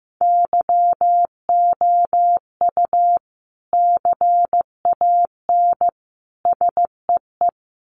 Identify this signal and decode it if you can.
Morse code, heard as a keyed tone, about 15 wpm, 700 Hz. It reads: YOU CAN SEE
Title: Sets of 3 Words 15wpm